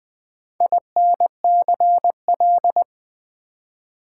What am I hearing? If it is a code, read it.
Morse code, heard as a keyed tone, about 20 wpm, 700 Hz. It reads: INCL
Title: Build A Word - 2 Letters At A Time - 8-Letter Words - 20wpm